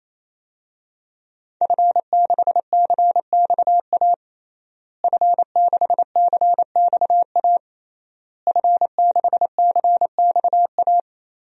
Morse code, heard as a keyed tone, about 28 wpm, 700 Hz. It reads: F6CXA F6CXA F6CXA
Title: DXCC-100 Call Signs - 5 Characters - Repeated 3x - 28wpm